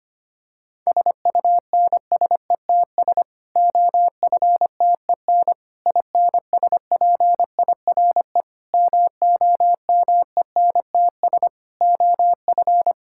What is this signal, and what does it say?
Morse code, heard as a keyed tone, about 25 wpm, 700 Hz. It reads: SUNSETS OFTEN INSPIRE MOMENTS OF
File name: practice / 25wpm / Difficult Sentences with Hints - 25wpm